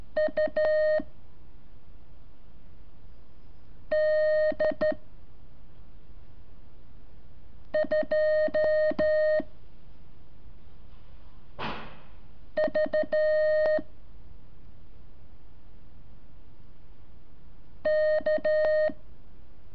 Morse code, heard as a keyed tone, about 13 wpm, 620 Hz. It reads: U D 2 V K
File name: curso / ejercicio 26